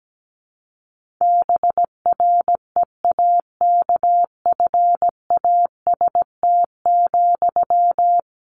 Morse code, heard as a keyed tone, about 17 wpm, 700 Hz. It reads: BREAKFAST,
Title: Sentences from Top 1500 Words 17wpm